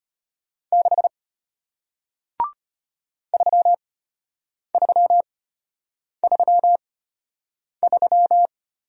Morse code, heard as a keyed tone, about 34 wpm, spaced as for 8 wpm, 700 Hz. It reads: 6 3333